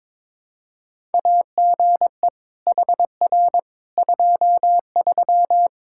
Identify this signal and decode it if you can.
Morse code, heard as a keyed tone, about 22 wpm, 700 Hz. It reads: AGE HR 23
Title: K1USN SST QSO Practice 22wpm